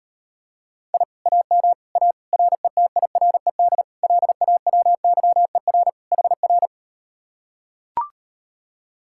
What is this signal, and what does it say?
Morse code, heard as a keyed tone, about 39 wpm, 700 Hz. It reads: I AM A RETIRED LAWYER HR